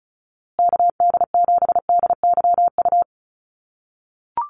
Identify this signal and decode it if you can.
Morse code, heard as a keyed tone, about 34 wpm, 700 Hz. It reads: KD7DYU
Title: USA Call Signs Repeated 3x 35wpm